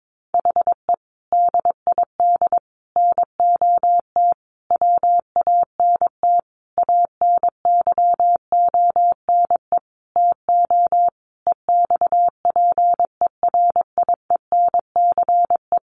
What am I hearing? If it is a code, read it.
Morse code, heard as a keyed tone, about 22 wpm, 700 Hz. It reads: HE DID NOT WANT ANYONE TO EXPERIENCE
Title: Sentences from Top 1100 Words 22wpm